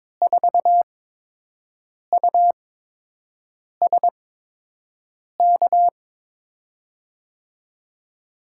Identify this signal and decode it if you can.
Morse code, heard as a keyed tone, about 22 wpm, spaced as for 6 wpm, 700 Hz. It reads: 4USK
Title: USA Call Signs - 6 Characters - 8x Character Spacing - 22wpm